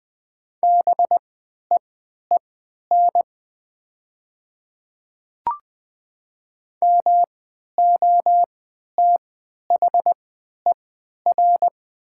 Morse code, heard as a keyed tone, about 20 wpm, spaced as for 11 wpm, 700 Hz. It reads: BEEN MOTHER
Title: 3x Character Spacing - Top 200 Words - 20wpm